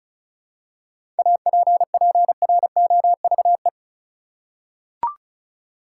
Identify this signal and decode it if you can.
Morse code, heard as a keyed tone, about 36 wpm, 700 Hz. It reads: APPROVE
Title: Seven Letter Words - 35wpm